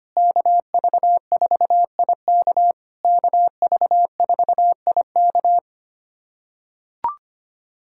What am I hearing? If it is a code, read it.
Morse code, heard as a keyed tone, about 25 wpm, 700 Hz. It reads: KV4IK KV4IK
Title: SOTA QSO Practice 25wpm